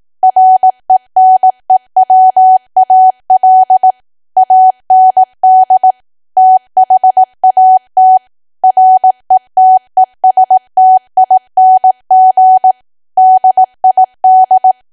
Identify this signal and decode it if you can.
Morse code, heard as a keyed tone, about 18 wpm, 750 Hz. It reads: RENEWAL AND THAT RETESTING DID
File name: cwa / W1AW-Level2 / 18WPM